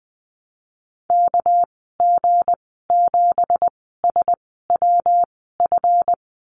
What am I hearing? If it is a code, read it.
Morse code, heard as a keyed tone, about 20 wpm, spaced as for 15 wpm, 700 Hz. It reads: KG7SWF
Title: 213 Lesson 34g - Number 8 - Call Signs 20wpm